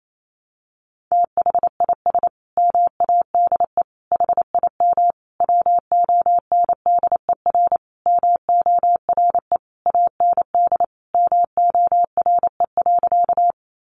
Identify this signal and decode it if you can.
Morse code, heard as a keyed tone, about 28 wpm, 700 Hz. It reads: THIS MADE HIM WONDER MORE AND MORE.
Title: Sentences from Top 500 Words 28wpm